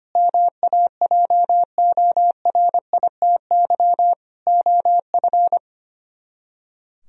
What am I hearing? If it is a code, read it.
Morse code, heard as a keyed tone, about 29 wpm, 700 Hz. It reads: MAJORITY OF
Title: Speed-Racing - Sets of 3 Words 25wpm